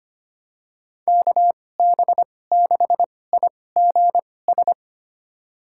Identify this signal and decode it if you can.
Morse code, heard as a keyed tone, about 25 wpm, spaced as for 18 wpm, 700 Hz. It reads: KB6IGS
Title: USA Call Signs - 6 Characters - 2x Character Spacing - 25wpm